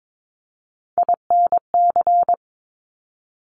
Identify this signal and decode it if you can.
Morse code, heard as a keyed tone, about 22 wpm, 700 Hz. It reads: INC
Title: Top Three Letter Combinations in Words 22wpm